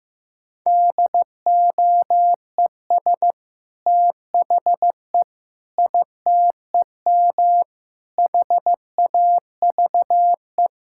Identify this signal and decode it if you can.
Morse code, heard as a keyed tone, about 15 wpm, 700 Hz. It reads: DOES THE ITEM HAVE
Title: Sentences from Top 1000 Words - Review Part 6 - 15wpm